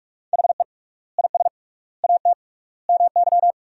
Morse code, heard as a keyed tone, about 46 wpm, 700 Hz. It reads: HE IS AT MY
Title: 3x Word-Spacing - Sentences from Top 200 Words - 45wpm